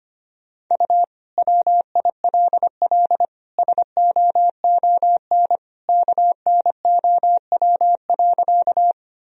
Morse code, heard as a keyed tone, about 25 wpm, 700 Hz. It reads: U WILL SOON KNOW.
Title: Sentences from Top 400 Words and Abbreviations 25wpm